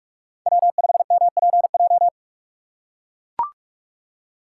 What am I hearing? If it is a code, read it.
Morse code, heard as a keyed tone, about 41 wpm, 700 Hz. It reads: W5MPJ